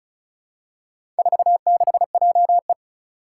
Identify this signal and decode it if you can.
Morse code, heard as a keyed tone, about 36 wpm, 700 Hz. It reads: 46JE
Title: Four Challenging Letter-Number Combinations 35wpm